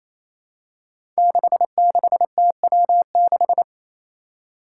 Morse code, heard as a keyed tone, about 28 wpm, 700 Hz. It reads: 66TW6
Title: Five Challenging Letter-Number Combinations 28wpm